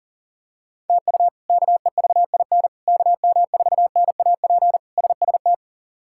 Morse code, heard as a keyed tone, about 40 wpm, 700 Hz. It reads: TU KEVIN KM4NAP SST
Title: K1USN SST QSO Practice - 40wpm